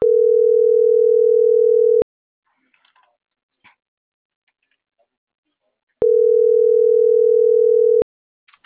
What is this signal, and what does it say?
Ringback tone (North America), 440 + 480 Hz.
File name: CENTRAL MABE / out karen.maya